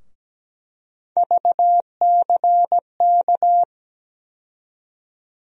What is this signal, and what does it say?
Morse code, heard as a keyed tone, about 17 wpm, 700 Hz. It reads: VCK
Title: USA Call Sign Postfixes - 3 Letters 17wpm